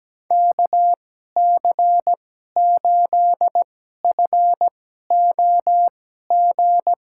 Morse code, heard as a keyed tone, about 17 wpm, spaced as for 12 wpm, 700 Hz. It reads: KC8FOG
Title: USA Call Signs - 6 Characters - 2x Character Spacing - 17wpm